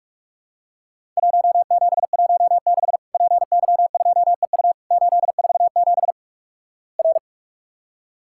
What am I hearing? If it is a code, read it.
Morse code, heard as a keyed tone, about 46 wpm, 700 Hz. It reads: 1716 PY2EU 847 R
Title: Worked All Europe DX Contest - QTC Exchange - 45wpm